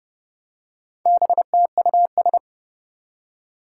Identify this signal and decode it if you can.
Morse code, heard as a keyed tone, about 30 wpm, 700 Hz. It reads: BTUS
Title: Four Challenging Letter-Number Combinations 30wpm